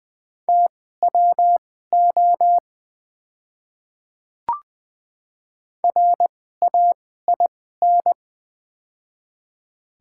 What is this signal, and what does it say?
Morse code, heard as a keyed tone, about 21 wpm, spaced as for 15 wpm, 700 Hz. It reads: TWO RAIN
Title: 2x Character Spacing - Top 300 Words - 20wpm